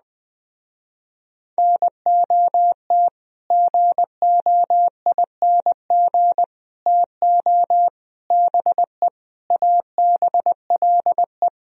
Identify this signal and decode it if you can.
Morse code, heard as a keyed tone, about 20 wpm, 700 Hz. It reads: NOT GOING TO BE ABLE